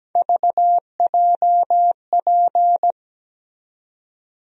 Morse code, heard as a keyed tone, about 17 wpm, 700 Hz. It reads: VJP
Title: USA Call Sign Postfixes - 3 Letters 17wpm